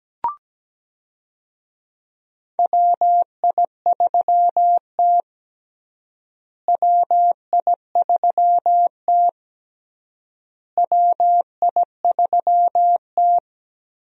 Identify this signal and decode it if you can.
Morse code, heard as a keyed tone, about 17 wpm, 700 Hz. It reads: WI3T WI3T WI3T
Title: USA Call Signs - 4 Characters - Repeated 3x - 17wpm